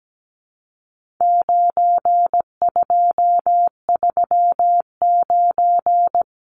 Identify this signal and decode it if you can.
Morse code, heard as a keyed tone, about 17 wpm, 700 Hz. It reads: 9239
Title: Numbers - 4-Digit 17wpm